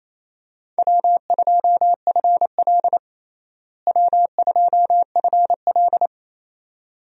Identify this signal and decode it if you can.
Morse code, heard as a keyed tone, about 28 wpm, 700 Hz. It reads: W2FL W2FL
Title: USA Call Signs - 4 Characters - Repeated 2x - 28wpm